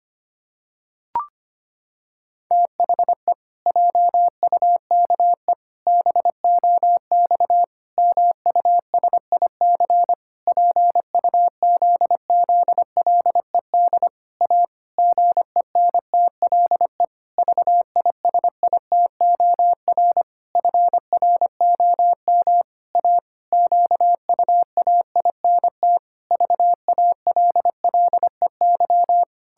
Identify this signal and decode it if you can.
Morse code, heard as a keyed tone, about 25 wpm, 700 Hz. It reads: THE JUKE BOX MUSIC PUZZLED A GENTLE VISITOR FROM A QUAINT VALLEY
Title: Pangram Challenge v2 - 25wpm